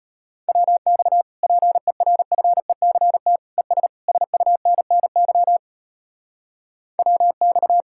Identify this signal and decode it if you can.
Morse code, heard as a keyed tone, about 33 wpm, 700 Hz. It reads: WX PERFECT ES SUNNY WX